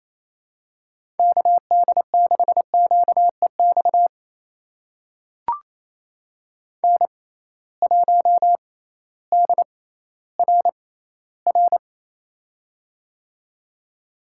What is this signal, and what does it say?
Morse code, heard as a keyed tone, about 28 wpm, spaced as for 15 wpm, 700 Hz. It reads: KD6QEX N1DRR
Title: USA Call Signs - 6x Character Spacing - 28wpm